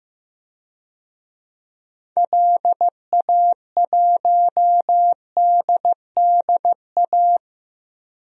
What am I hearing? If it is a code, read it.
Morse code, heard as a keyed tone, about 15 wpm, 700 Hz. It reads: LA1DDA
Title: DXCC-100 Call Signs - 6 Characters - 15wpm